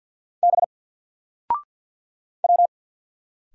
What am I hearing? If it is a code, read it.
Morse code, heard as a keyed tone, about 50 wpm, 700 Hz. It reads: B W
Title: Single Letter-Number 50wpm